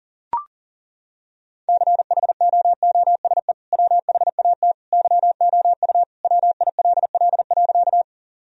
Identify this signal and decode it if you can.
Morse code, heard as a keyed tone, about 40 wpm, 700 Hz. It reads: CHOOSE WHAT YOU WILL.
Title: Sentences from Top 1000 Words - Review Part 4 - 40wpm